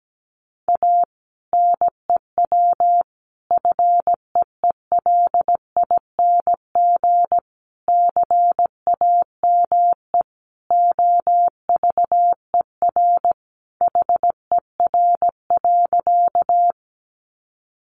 Morse code, heard as a keyed tone, about 17 wpm, 700 Hz. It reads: A NEW FEELING CAME OVER HER.
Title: Sentences from Top 1200 Words 17wpm